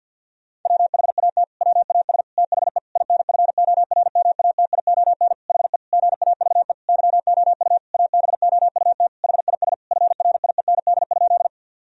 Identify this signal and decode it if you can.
Morse code, heard as a keyed tone, about 50 wpm, 700 Hz. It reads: WHAT WAS THE INFORMATION HE GAVE YOU ABOUT HIS FRIEND?
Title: Sentences from Top 1000 Words - Review Part 3 - 50wpm